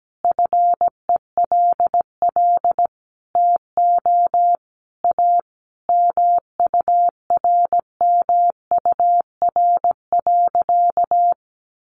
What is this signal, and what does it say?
Morse code, heard as a keyed tone, about 17 wpm, 700 Hz. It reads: FELL TO A MURMUR.
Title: A Sound Of Thunder 17wpm